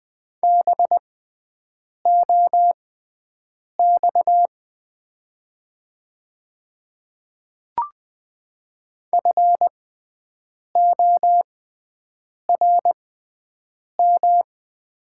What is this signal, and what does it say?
Morse code, heard as a keyed tone, about 20 wpm, spaced as for 7 wpm, 700 Hz. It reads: BOX FORM
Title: 6x Character Spacing - Top 500 Words - 20wpm